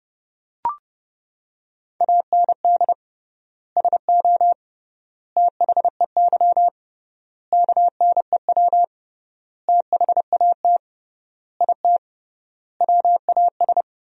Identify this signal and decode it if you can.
Morse code, heard as a keyed tone, about 30 wpm, 700 Hz. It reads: AND SO THEY KNEW THAT IT WAS
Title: 3x Word-Spacing - Sentences from Top 400 Words - 30wpm